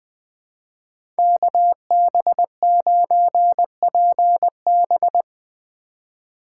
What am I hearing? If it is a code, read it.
Morse code, heard as a keyed tone, about 22 wpm, 700 Hz. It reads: KB9PB
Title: Speed Racing - USA 2x2 Call Signs 20wpm